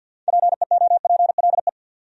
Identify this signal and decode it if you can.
Morse code, heard as a keyed tone, about 68 wpm, spaced as for 59 wpm, 700 Hz. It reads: PEOPLE
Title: Speed-Racing Top 300 Words 50wpm